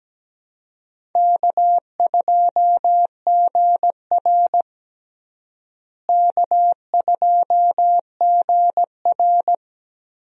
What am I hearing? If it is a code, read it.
Morse code, heard as a keyed tone, about 17 wpm, 700 Hz. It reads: K2GR K2GR
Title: USA Call Signs - 4 Characters - Repeated 2x - 17wpm